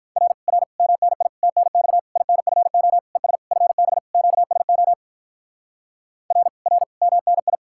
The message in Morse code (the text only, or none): R R MNI TNX INFO ES FB QSO R R MNI